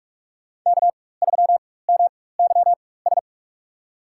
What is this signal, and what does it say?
Morse code, heard as a keyed tone, about 46 wpm, spaced as for 21 wpm, 700 Hz. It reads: K3MYS
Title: USA Call Signs - 5 Characters - 4x Character Spacing - 45wpm